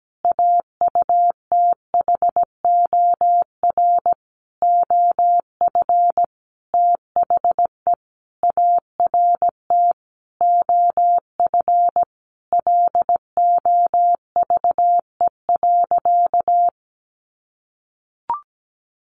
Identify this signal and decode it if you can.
Morse code, heard as a keyed tone, about 17 wpm, 700 Hz. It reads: AUTHOR OF THE ART OF LOVE.
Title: Sentences from Top 1500 Words 17wpm